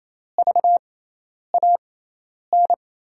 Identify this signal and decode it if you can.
Morse code, heard as a keyed tone, about 28 wpm, spaced as for 10 wpm, 700 Hz. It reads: VAN
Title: USA Call Signs - 6 Characters - 6x Character Spacing - 28wpm